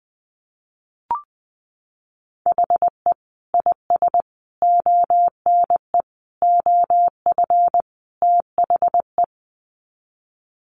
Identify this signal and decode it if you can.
Morse code, heard as a keyed tone, about 20 wpm, 700 Hz. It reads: HE IS ONE OF THE
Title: Sets of 5 Words Encore 20wpm